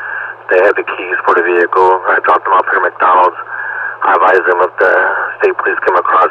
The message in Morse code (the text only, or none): none